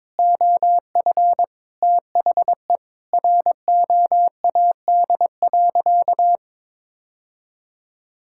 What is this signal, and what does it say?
Morse code, heard as a keyed tone, about 22 wpm, 700 Hz. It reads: OF THE ROAD.
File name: practice / 22wpm / Sentences with 3-Word Phrases - 22wpm